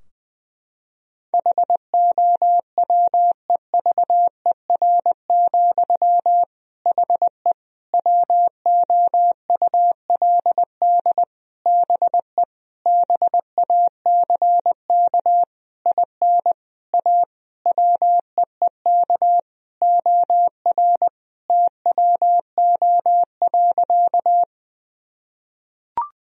Morse code, heard as a keyed tone, about 21 wpm, 700 Hz. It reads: HOWEVER, HE WOULD BE BACK IN A WEEK OR TWO.
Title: Sentences from Top 1000 Words - Review Part 2 - 20wpm